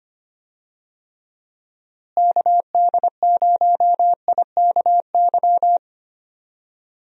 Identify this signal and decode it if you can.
Morse code, heard as a keyed tone, about 25 wpm, 700 Hz. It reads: KD0IKY
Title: USA Call Signs - 6 Characters - Repeated 3x - 25wpm